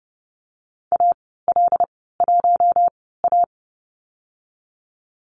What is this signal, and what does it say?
Morse code, heard as a keyed tone, about 30 wpm, spaced as for 17 wpm, 700 Hz. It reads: AL1A
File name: Lesson 15g - Number 5 - Call Signs 30wpm